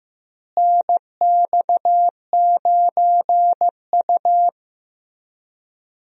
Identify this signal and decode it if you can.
Morse code, heard as a keyed tone, about 15 wpm, 700 Hz. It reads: NX9U
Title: USA Call Signs - 4 Characters - Repeated 3x - 15wpm